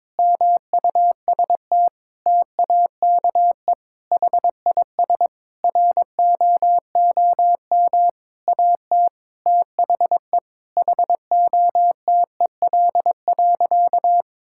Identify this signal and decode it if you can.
Morse code, heard as a keyed tone, about 21 wpm, 700 Hz. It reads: MUST TAKE HIS ROOM AT THE HOTEL.
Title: Sentences from Top 1600 Words 22wpm